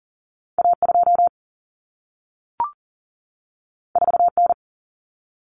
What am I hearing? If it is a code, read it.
Morse code, heard as a keyed tone, about 40 wpm, 700 Hz. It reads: A2 4N
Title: Common Cut Numbers - 2-Digit 40wpm